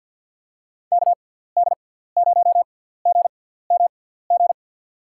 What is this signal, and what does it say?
Morse code, heard as a keyed tone, about 52 wpm, spaced as for 17 wpm, 700 Hz. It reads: KD0GMG
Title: USA Call Signs - 6x Character Spacing - 50wpm